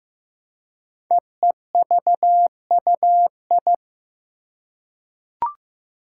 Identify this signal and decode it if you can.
Morse code, heard as a keyed tone, about 15 wpm, 700 Hz. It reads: EEVUI
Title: Code Groups - 5 Letters - 15wpm